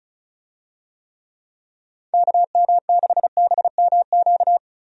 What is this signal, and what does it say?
Morse code, heard as a keyed tone, about 34 wpm, 700 Hz. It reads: KM6BMQ
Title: USA Call Signs - 6 Characters - 35wpm